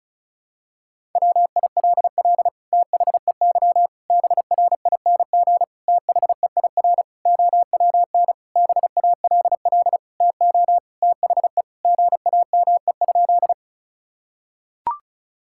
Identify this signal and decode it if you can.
Morse code, heard as a keyed tone, about 34 wpm, 700 Hz. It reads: WILL THEY BRING THEIR OWN BALL TO THE GAME?
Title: Sentences from Top 1000 Words - Review Part 6 - 35wpm